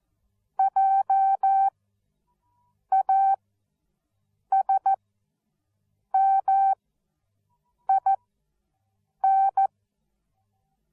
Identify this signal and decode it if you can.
Morse code, heard as a keyed tone, about 14 wpm, spaced as for 6 wpm, 780 Hz. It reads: JASMIN
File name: Onlinelehrgang / morsen / DJ4UF Morsen13